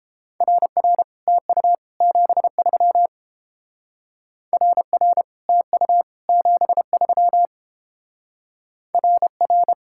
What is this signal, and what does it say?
Morse code, heard as a keyed tone, about 29 wpm, 700 Hz. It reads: RR TU 73 RR TU 73 RR